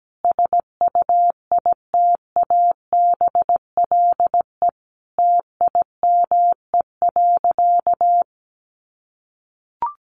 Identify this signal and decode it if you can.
Morse code, heard as a keyed tone, about 17 wpm, 700 Hz. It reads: SUITABLE TIME.
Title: Sentences from Top 1400 Words 17wpm